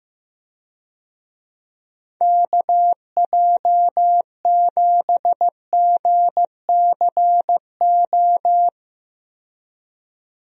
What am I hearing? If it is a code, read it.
Morse code, heard as a keyed tone, about 15 wpm, 700 Hz. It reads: KJ7GCO
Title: USA Call Signs - 6 Characters - 15wpm